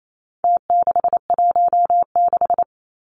Morse code, heard as a keyed tone, about 28 wpm, 700 Hz. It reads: T616
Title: Numbers - Contest Style - 4-Digit 28wpm